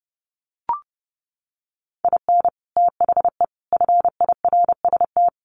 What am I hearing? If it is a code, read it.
Morse code, heard as a keyed tone, about 30 wpm, 700 Hz. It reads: IN THE FIRST
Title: Sets of 3 Words 30wpm